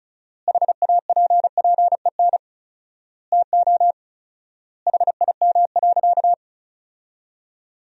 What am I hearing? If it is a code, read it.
Morse code, heard as a keyed tone, about 36 wpm, 700 Hz. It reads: HAPPEN TO HIM.
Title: 4x Word-Spacing - Sentences from Top 400 Words - 35wpm